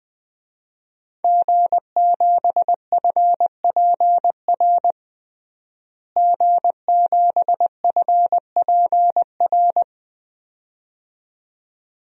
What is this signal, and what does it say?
Morse code, heard as a keyed tone, about 20 wpm, 700 Hz. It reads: G7FPR G7FPR
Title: DXCC-100 Call Signs - 5 Characters - Repeated 2x - 20wpm